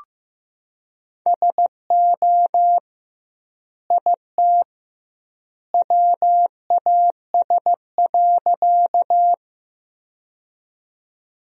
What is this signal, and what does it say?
Morse code, heard as a keyed tone, about 15 wpm, 700 Hz. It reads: SO IT WAS.
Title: 2x Word-Spacing - Sentences from Top 100 Words - 15wpm